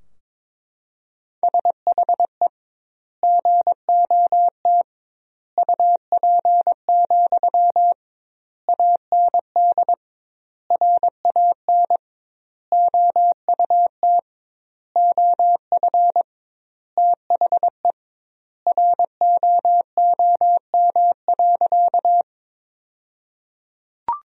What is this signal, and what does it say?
Morse code, heard as a keyed tone, about 22 wpm, 700 Hz. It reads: SHE GOT UP, AND RAN OUT OF THE ROOM.
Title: 2x Word-Spacing - Sentences from Top 500 Words - 22wpm